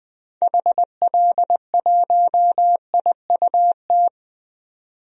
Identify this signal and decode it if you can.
Morse code, heard as a keyed tone, about 21 wpm, 700 Hz. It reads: HL1IUT
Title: DXCC-100 Call Signs - 6 Characters - 20wpm